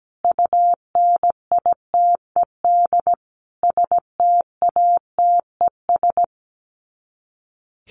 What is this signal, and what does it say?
Morse code, heard as a keyed tone, about 17 wpm, 700 Hz. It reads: UNITED STATES
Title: Sets of 5 Words 17wpm